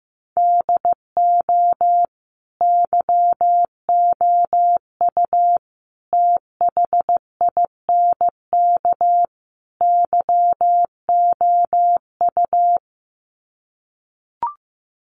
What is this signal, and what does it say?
Morse code, heard as a keyed tone, about 15 wpm, 700 Hz. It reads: DO YOU THINK YOU
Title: Sets of 4 Words 15wpm